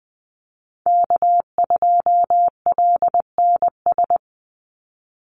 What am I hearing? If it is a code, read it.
Morse code, heard as a keyed tone, about 20 wpm, 700 Hz. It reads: K2LNS
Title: Call Signs - 20wpm